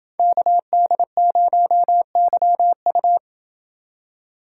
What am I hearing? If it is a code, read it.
Morse code, heard as a keyed tone, about 28 wpm, 700 Hz. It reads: KD0YU